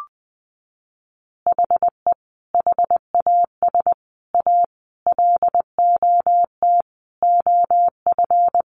Morse code, heard as a keyed tone, about 20 wpm, 700 Hz. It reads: HE HAS A LOT OF
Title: Additional Easy Phrases 20wpm